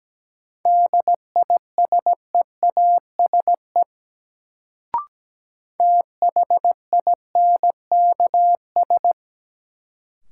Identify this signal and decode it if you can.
Morse code, heard as a keyed tone, about 17 wpm, 700 Hz. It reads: DISEASE THINKS
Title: Sentences from Top 1000 Words - Review Part 6 - New Word Drill - 17wpm